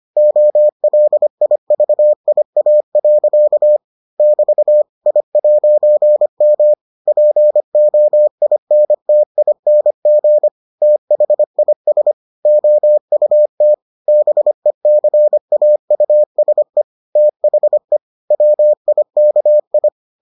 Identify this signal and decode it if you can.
Morse code, heard as a keyed tone, about 25 wpm, 600 Hz. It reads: OLIVIA. <BT> I'M POINTING THIS OUT BECAUSE THE WIKI